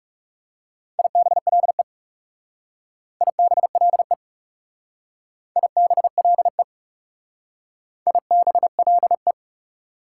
Morse code, heard as a keyed tone, about 35 wpm, 690 Hz. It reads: IBLE IBLE IBLE IBLE
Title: Speed-Racing Prefixes and Suffixes 30wpm